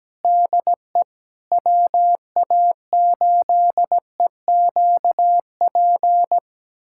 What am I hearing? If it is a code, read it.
Morse code, heard as a keyed tone, about 17 wpm, 700 Hz. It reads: DE WA8EQP
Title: USA Callsign DE Callsign - 6 Characters - 17wpm